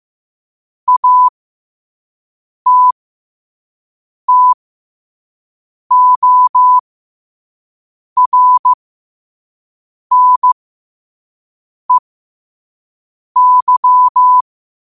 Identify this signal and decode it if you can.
Morse code, heard as a keyed tone, about 15 wpm, spaced as for 5 wpm, 1000 Hz. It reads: ATTORNEY
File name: qotdfiles / 5wpm